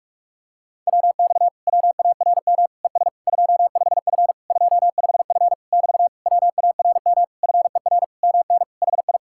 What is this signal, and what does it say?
Morse code, heard as a keyed tone, about 49 wpm, 700 Hz. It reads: WX WARM ES 25F 25F <BT> WARM FER MN HI